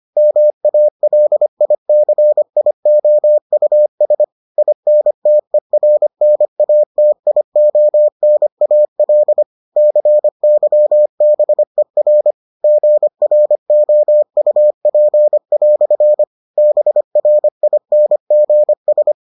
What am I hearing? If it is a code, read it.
Morse code, heard as a keyed tone, about 25 wpm, 600 Hz. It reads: MALICIOUS INTERNATIONAL CYBER GROUP" BRINGS